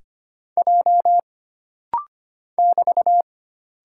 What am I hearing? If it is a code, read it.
Morse code, heard as a keyed tone, about 25 wpm, 700 Hz. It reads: J <BT>
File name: Single Character (Letter:Number:Punctuation:Prosign) - Rapid-Fire - 25wpm